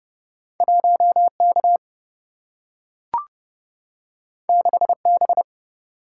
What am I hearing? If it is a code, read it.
Morse code, heard as a keyed tone, about 30 wpm, 700 Hz. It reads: 1K 6B
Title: Single-Letter Single-Number 30wpm